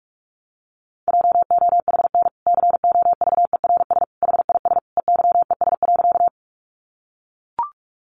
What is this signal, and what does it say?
Morse code, heard as a keyed tone, about 44 wpm, 700 Hz. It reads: JOHN COVERS HIS EYES.
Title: Easy Phrases 45wpm